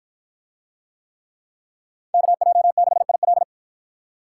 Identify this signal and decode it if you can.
Morse code, heard as a keyed tone, about 48 wpm, 700 Hz. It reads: KJ6IL